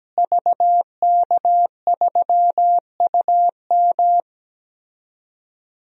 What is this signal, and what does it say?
Morse code, heard as a keyed tone, about 17 wpm, 700 Hz. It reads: VK3UM
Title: DXCC-100 Call Signs - 5 Characters - Repeated 3x - 17wpm